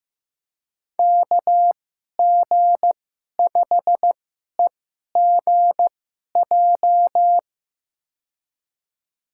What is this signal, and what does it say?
Morse code, heard as a keyed tone, about 15 wpm, spaced as for 11 wpm, 700 Hz. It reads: KG5EGJ
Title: USA Call Signs - 2x Character Spacing - 15wpm